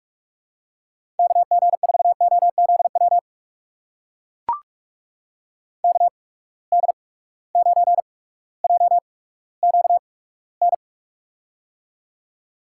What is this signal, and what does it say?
Morse code, heard as a keyed tone, about 44 wpm, spaced as for 19 wpm, 700 Hz. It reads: KG4OZW KD9JQN
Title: USA Call Signs - 8x Character Spacing - 45wpm